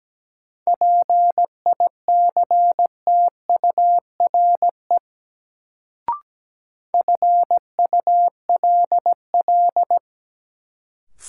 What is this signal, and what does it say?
Morse code, heard as a keyed tone, about 17 wpm, 700 Hz. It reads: PICTURE FULL
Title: Sentences from Top 500 Words - New Word Drill - 17wpm